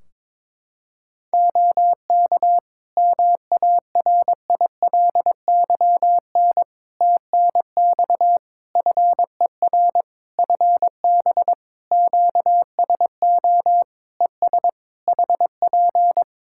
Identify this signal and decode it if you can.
Morse code, heard as a keyed tone, about 22 wpm, 700 Hz. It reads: OK MARILYN TNX FER FB QSO ES HP
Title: Practice QSO - Exchange 4 - Ending - 22wpm